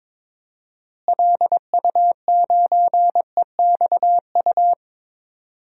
Morse code, heard as a keyed tone, about 22 wpm, 700 Hz. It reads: LU9EXU
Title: DXCC-100 Call Signs - 6 Characters - Repeated 3x - 22wpm